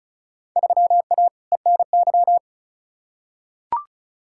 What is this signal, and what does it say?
Morse code, heard as a keyed tone, about 36 wpm, 700 Hz. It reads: 3A ENY
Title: Field Day Exchanges - 35wpm